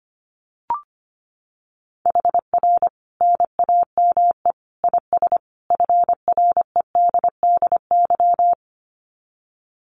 Morse code, heard as a keyed tone, about 25 wpm, 700 Hz. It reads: HR NAME IS FREDDY
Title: Common QSO Phrases 25wpm